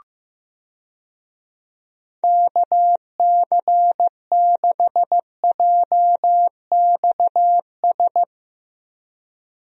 Morse code, heard as a keyed tone, about 15 wpm, 700 Hz. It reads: KC6JXS
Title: USA Call Signs - 6 Characters - Repeated 2x - 15wpm